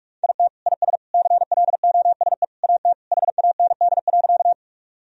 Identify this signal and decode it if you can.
Morse code, heard as a keyed tone, about 46 wpm, 700 Hz. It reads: IT IS CLOSE AT HAND.
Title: Sentences from Top 1000 Words - Review Part 1 - 45wpm